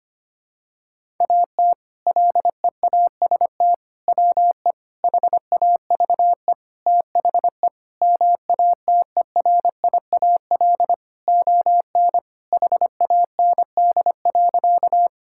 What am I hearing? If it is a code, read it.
Morse code, heard as a keyed tone, about 25 wpm, 700 Hz. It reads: AT LEAST WE HAVE THE MATERIAL ON HAND.
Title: Sentences from Top 1000 Words - Review Part 5 - 25wpm